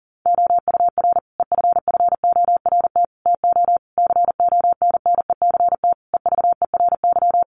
Morse code, heard as a keyed tone, about 40 wpm, 700 Hz. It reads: OUR EFFORT TO CONNECT EVERY
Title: Sentences from Top 1500 Words 40wpm